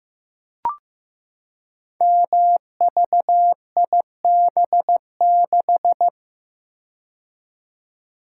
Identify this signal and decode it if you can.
Morse code, heard as a keyed tone, about 15 wpm, 700 Hz. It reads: MVIB6
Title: Code Groups - 5 Letter:Numbers - 15wpm